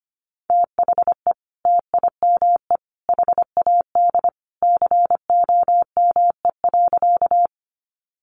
Morse code, heard as a keyed tone, about 25 wpm, 700 Hz. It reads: THE TIME HAD COME.
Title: Sentences from Top 100 Words 25wpm